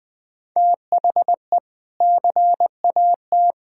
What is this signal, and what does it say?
Morse code, heard as a keyed tone, about 20 wpm, 700 Hz. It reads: THE CAT
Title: Sentences with 2-Word Phrases - Spoken Beginning - 20wpm